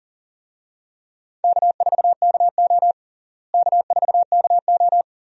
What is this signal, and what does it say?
Morse code, heard as a keyed tone, about 40 wpm, 700 Hz. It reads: K4KO K4KO
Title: USA Call Signs - 4 Characters - Repeated 2x - 40wpm